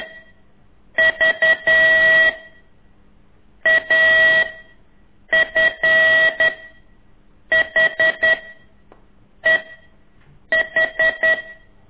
Morse code, heard as a keyed tone, about 11 wpm, 1950 Hz. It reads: V A F H E H
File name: ik1weg2 / CORSI CW